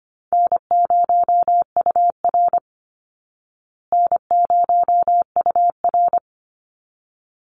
Morse code, heard as a keyed tone, about 25 wpm, 700 Hz. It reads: N0UR N0UR
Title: Easy Call Signs Repeated 2x - 25wpm